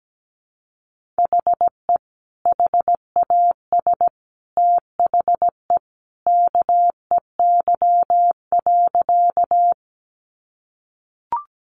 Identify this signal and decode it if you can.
Morse code, heard as a keyed tone, about 17 wpm, 700 Hz. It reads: HE HAS THE KEY.
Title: Sentences from Top 600 Words 17wpm